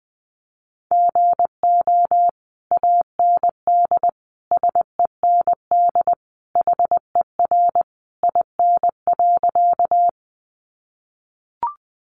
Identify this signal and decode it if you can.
Morse code, heard as a keyed tone, about 20 wpm, 700 Hz. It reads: GO AND SEND HER IN.
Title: Sentences from Top 1500 Words 20wpm